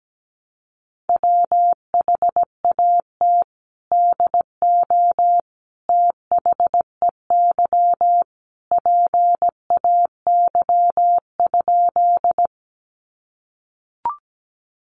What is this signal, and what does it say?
Morse code, heard as a keyed tone, about 17 wpm, 700 Hz. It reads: WHAT DO THEY PAY?
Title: Sentences from Top 700 Words 17wpm